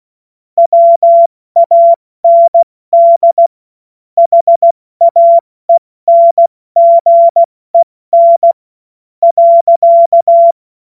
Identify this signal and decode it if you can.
Morse code, heard as a keyed tone, about 16 wpm, spaced as for 13 wpm, 680 Hz. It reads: WAND HAENGEN.
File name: Morsekurs / Witze / Witz 28. September 2021, 70 F80